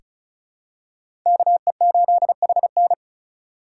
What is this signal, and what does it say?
Morse code, heard as a keyed tone, about 36 wpm, 700 Hz. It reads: KE8HN
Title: USA Call Signs - 5 Characters - 3x Character Spacing - 35wpm